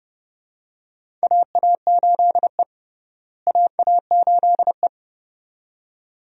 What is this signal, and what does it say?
Morse code, heard as a keyed tone, about 30 wpm, 700 Hz. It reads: AA8E AA8E
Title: USA Call Signs - 4 Characters - Repeated 2x - 30wpm